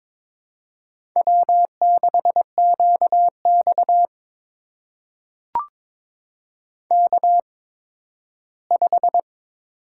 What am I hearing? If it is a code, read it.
Morse code, heard as a keyed tone, about 22 wpm, spaced as for 13 wpm, 700 Hz. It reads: W6QX K5
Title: USA Call Signs - 8x Character Spacing - 22wpm